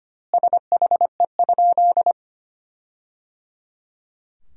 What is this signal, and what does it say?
Morse code, heard as a keyed tone, about 25 wpm, 700 Hz. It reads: SHE?
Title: 2x Word-Spacing - Sentences from Top 100 Words - 25wpm